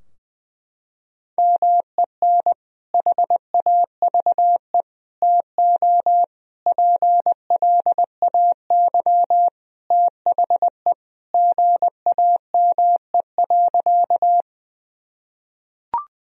Morse code, heard as a keyed tone, about 20 wpm, 700 Hz. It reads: MEN HAVE TO PLAY THE GAME.
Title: Sentences from Top 1000 Words - Review Part 3 - 20wpm